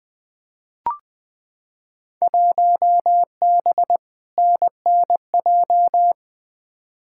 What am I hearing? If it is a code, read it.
Morse code, heard as a keyed tone, about 21 wpm, 700 Hz. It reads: 1B NNJ
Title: Field Day Exchanges - 20wpm